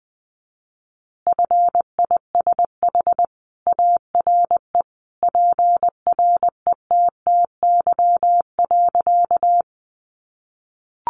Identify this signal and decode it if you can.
Morse code, heard as a keyed tone, about 20 wpm, 700 Hz. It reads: FISH ARE PRETTY.
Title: Additional Easy Phrases 20wpm